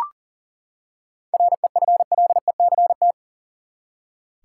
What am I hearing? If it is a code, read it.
Morse code, heard as a keyed tone, about 40 wpm, 700 Hz. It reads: REFLECT
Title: Seven Letter Words - 40wpm